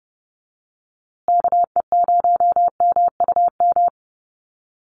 Morse code, heard as a keyed tone, about 30 wpm, 700 Hz. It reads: KE0MUM
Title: USA Call Signs Repeated 3x 30wpm